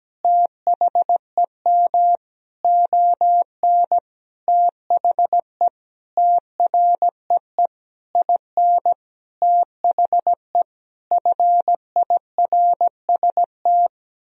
Morse code, heard as a keyed tone, about 17 wpm, 700 Hz. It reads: THEM ON THE TREE IN THE FIRST
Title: Sentences from Top 1000 Words - Review Part 1 - 17wpm